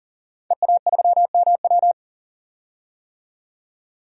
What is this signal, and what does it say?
Morse code, heard as a keyed tone, about 40 wpm, 700 Hz. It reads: EA3MW
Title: DXCC-100 Call Signs - 5 Characters - 40wpm